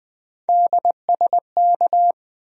Morse code, heard as a keyed tone, about 20 wpm, 700 Hz. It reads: DSK
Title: USA Call Sign Postfixes - 3 Letters 20wpm